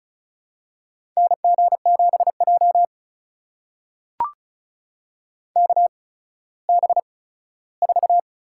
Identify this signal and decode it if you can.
Morse code, heard as a keyed tone, about 36 wpm, spaced as for 17 wpm, 700 Hz. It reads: NG7J KB4
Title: USA Call Signs - 8x Character Spacing - 35wpm